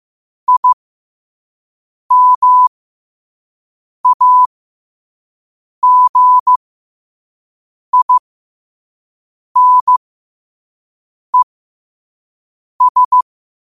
Morse code, heard as a keyed tone, about 15 wpm, spaced as for 5 wpm, 1000 Hz. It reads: IMAGINES